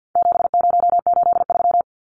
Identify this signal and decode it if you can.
Morse code, heard as a keyed tone, about 50 wpm, 700 Hz. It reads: 7083
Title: Numbers - 4-Digit 50wpm